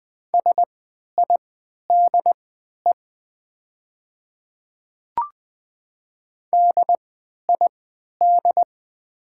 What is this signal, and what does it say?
Morse code, heard as a keyed tone, about 20 wpm, spaced as for 11 wpm, 700 Hz. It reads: SIDE DID
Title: 3x Character Spacing - Top 100 Words - 20wpm